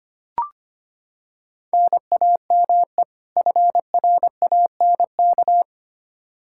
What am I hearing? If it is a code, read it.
Morse code, heard as a keyed tone, about 25 wpm, 700 Hz. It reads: NAME FRANK
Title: K1USN SST QSO Practice 25wpm